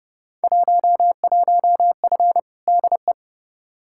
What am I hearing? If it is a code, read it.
Morse code, heard as a keyed tone, about 30 wpm, 700 Hz. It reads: 11F DE
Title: Field Day Exchanges Encore - 30wpm